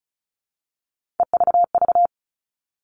Morse code, heard as a keyed tone, about 34 wpm, 700 Hz. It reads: EVV
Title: All Cut Numbers - 3-Digit 35wpm